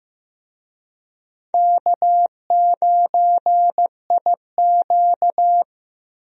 Morse code, heard as a keyed tone, about 15 wpm, 700 Hz. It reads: K9IQ
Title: USA Call Signs - 4 Characters - Repeated 2x - 15wpm